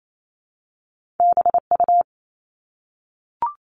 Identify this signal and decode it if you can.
Morse code, heard as a keyed tone, about 28 wpm, 700 Hz. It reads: BU
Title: Top Two Letter Combinations in Words 28wpm